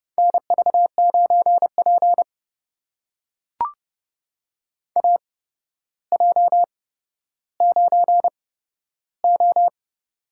Morse code, heard as a keyed tone, about 31 wpm, spaced as for 13 wpm, 700 Hz. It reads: NV9P AJ9O
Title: USA Call Signs - 4 Characters - 8x Character Spacing - 30wpm